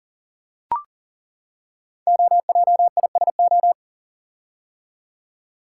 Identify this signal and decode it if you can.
Morse code, heard as a keyed tone, about 41 wpm, 700 Hz. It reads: OJISO
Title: Code Groups - 5 Letters - 40wpm